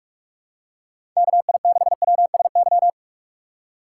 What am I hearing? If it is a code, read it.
Morse code, heard as a keyed tone, about 46 wpm, 700 Hz. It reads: KI6WSY
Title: USA Call Signs - 6 Characters - 2x Character Spacing - 45wpm